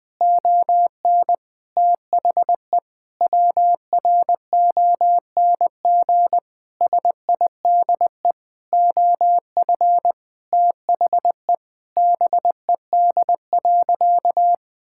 Morse code, heard as a keyed tone, about 20 wpm, 700 Hz. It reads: ON THE WRONG SIDE OF THE BED.
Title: Sentences with 6-Word Phrases - 20wpm